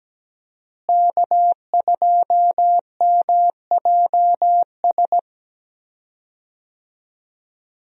Morse code, heard as a keyed tone, about 17 wpm, 700 Hz. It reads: K2MJS
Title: USA Call Signs - 5 Characters - Repeated 3x - 17wpm